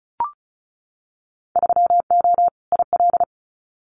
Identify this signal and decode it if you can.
Morse code, heard as a keyed tone, about 36 wpm, 690 Hz. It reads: 3O IL
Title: Winter Field Day Exchanges 35wpm